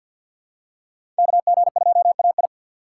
Morse code, heard as a keyed tone, about 48 wpm, 700 Hz. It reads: KG2AI
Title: USA Call Signs - 5 Characters - 4x Character Spacing - 50wpm